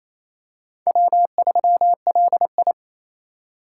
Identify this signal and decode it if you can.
Morse code, heard as a keyed tone, about 28 wpm, 700 Hz. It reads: W3LI
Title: USA Call Signs - 3x Character Spacing - 28wpm